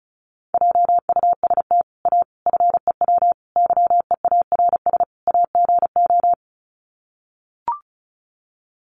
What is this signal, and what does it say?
Morse code, heard as a keyed tone, about 34 wpm, 700 Hz. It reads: JUST A FEW YEARS AGO
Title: Sets of 5 Words Encore 35wpm